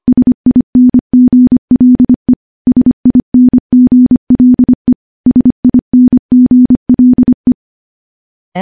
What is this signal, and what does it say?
Morse code, heard as a keyed tone, about 25 wpm, 250 Hz. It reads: SINGLE SINGLE SINGLE
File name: morse / morse-training-25WPM